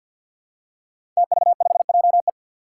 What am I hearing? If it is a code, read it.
Morse code, heard as a keyed tone, about 47 wpm, 700 Hz. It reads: TV5JE
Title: Five Challenging Letter-Number Combinations 50wpm